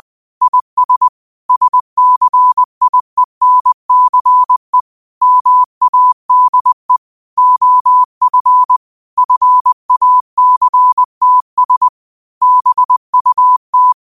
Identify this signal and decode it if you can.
Morse code, heard as a keyed tone, about 20 wpm, 1000 Hz. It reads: IS SCIENCE MADE OF FACTS BUT
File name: qotdfiles / 20wpm